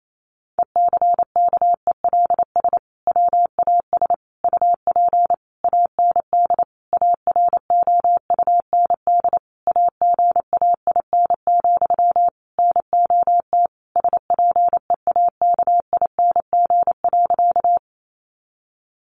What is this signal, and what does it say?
Morse code, heard as a keyed tone, about 28 wpm, 700 Hz. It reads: ECKELS WAS UP AND AROUND AGAIN, NOT SPEAKING.
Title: A Sound Of Thunder 28wpm